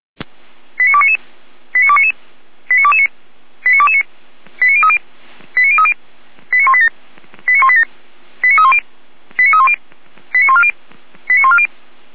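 Five-tone selective call (ZVEI), heard as quick runs of five tones, tones of about 70 ms. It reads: 89299 89299 89229 89229 89030 89030 89187 89187 89310 89310 89140 89140
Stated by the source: ZVEI five-tone selective call 89165 received via SDR.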